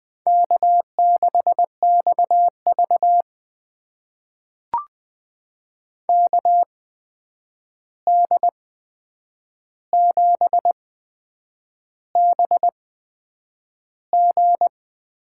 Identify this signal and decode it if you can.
Morse code, heard as a keyed tone, about 20 wpm, spaced as for 8 wpm, 700 Hz. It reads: K6XV KD7BG
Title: USA Call Signs - 8x Character Spacing - 20wpm